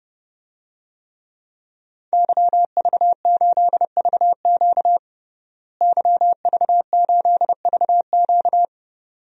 Morse code, heard as a keyed tone, about 32 wpm, 700 Hz. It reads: YV8VQ YV8VQ
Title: DXCC-100 Call Signs - 5 Characters - Repeated 2x - 30wpm